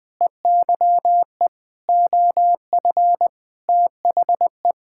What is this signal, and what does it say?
Morse code, heard as a keyed tone, about 20 wpm, 700 Hz. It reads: EYE OF THE
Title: Sentences with 7-Word Phrases - 20wpm